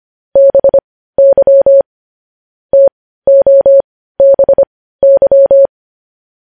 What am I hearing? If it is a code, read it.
Morse code, heard as a keyed tone, about 25 wpm, spaced as for 15 wpm, 550 Hz. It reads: BY TOBY
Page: news-headlines-in-morse-code-at-15-wpm